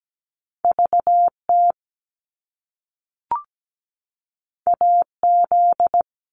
Morse code, heard as a keyed tone, about 17 wpm, 700 Hz. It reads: VT AZ
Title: US States 17wpm